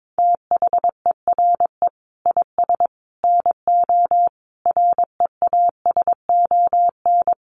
Morse code, heard as a keyed tone, about 22 wpm, 700 Hz. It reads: THERE IS NO REASON
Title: Sets of 5 Words Encore 22wpm